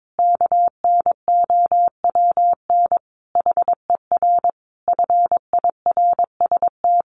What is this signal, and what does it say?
Morse code, heard as a keyed tone, about 22 wpm, 700 Hz. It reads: KNOWN HER FIRST
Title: Sentences from Top 1200 Words 22wpm